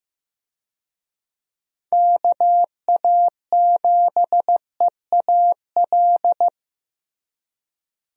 Morse code, heard as a keyed tone, about 15 wpm, 700 Hz. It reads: KA7EAL
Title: USA Call Signs - 6 Characters - Repeated 3x - 15wpm